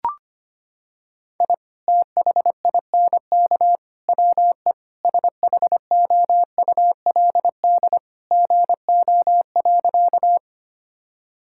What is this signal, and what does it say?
Morse code, heard as a keyed tone, about 25 wpm, 700 Hz. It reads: I THINK WE SHOULD GO.
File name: practice / 25wpm / Sentences with 3-Word Phrases - 25wpm